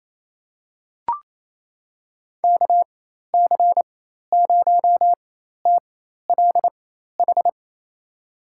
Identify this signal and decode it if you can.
Morse code, heard as a keyed tone, about 28 wpm, spaced as for 13 wpm, 700 Hz. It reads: KC0TLH
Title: USA Call Signs - 6 Characters - 4x Character Spacing - 28wpm